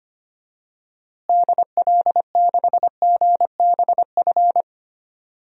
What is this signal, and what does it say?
Morse code, heard as a keyed tone, about 25 wpm, 700 Hz. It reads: DL6GBF
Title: DXCC-100 Call Signs - 6 Characters - Repeated 2x - 25wpm